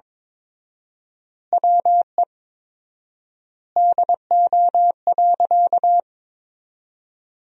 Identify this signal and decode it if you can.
Morse code, heard as a keyed tone, about 21 wpm, 700 Hz. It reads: WE DO.
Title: 4x Word-Spacing - Sentences from Top 100 Words - 22wpm